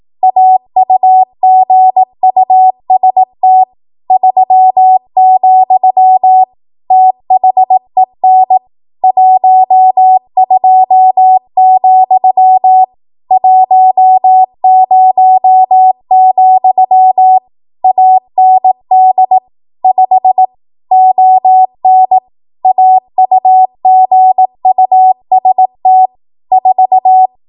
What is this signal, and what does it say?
Morse code, heard as a keyed tone, about 18 wpm, 750 Hz. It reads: AUGUST 3, THEN 12, 10, AND 5 ON AUGUST 4
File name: Archive / 18 WPM / 18WPM